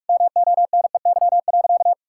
Morse code, heard as a keyed tone, about 41 wpm, 700 Hz. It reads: MONEY.
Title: Sentences from Top 1000 Words - Review Part 6 - 45wpm